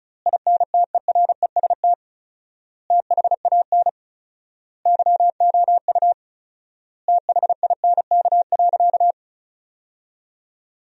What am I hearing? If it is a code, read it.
Morse code, heard as a keyed tone, about 36 wpm, 700 Hz. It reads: INTEREST THAN YOU THINK.
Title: 4x Word-Spacing - Sentences from Top 400 Words - 35wpm